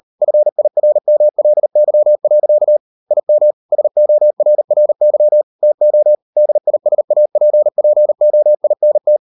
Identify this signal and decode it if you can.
Morse code, heard as a keyed tone, about 38 wpm, 600 Hz. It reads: FIRMLY. IM SORRY TO DISAPPOINT